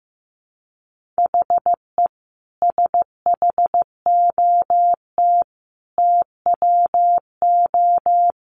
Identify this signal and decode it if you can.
Morse code, heard as a keyed tone, about 15 wpm, 700 Hz. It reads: HE SHOT TWO
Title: Sentences from Top 1100 Words 15wpm